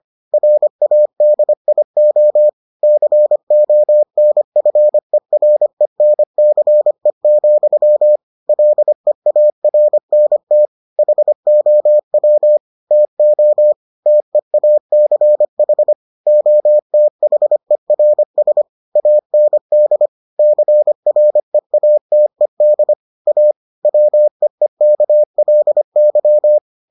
Morse code, heard as a keyed tone, about 25 wpm, 600 Hz. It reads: RADIO CONFERENCE, LEARNT HOW TO TEACH OTHERS AND CREATED A WEEKLY